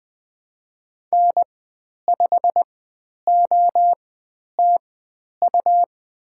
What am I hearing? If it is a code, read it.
Morse code, heard as a keyed tone, about 20 wpm, spaced as for 10 wpm, 700 Hz. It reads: N5OTU
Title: 102 Lesson 17g - Letter u - Call Signs 20wpm